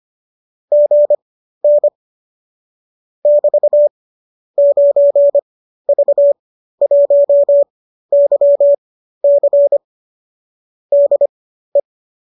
Morse code, heard as a keyed tone, about 25 wpm, spaced as for 13 wpm, 570 Hz. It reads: GN <BT> 9V1YC DE
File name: qso207 13